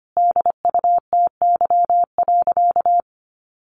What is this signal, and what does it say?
Morse code, heard as a keyed tone, about 25 wpm, 700 Hz. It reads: DUTY.
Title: Sentences from Top 1500 Words 25wpm